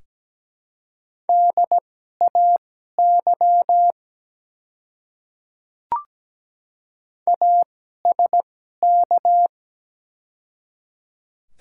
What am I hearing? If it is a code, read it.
Morse code, heard as a keyed tone, about 17 wpm, spaced as for 12 wpm, 700 Hz. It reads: DAY ASK
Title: 2x Character Spacing - Top 200 Words - 17wpm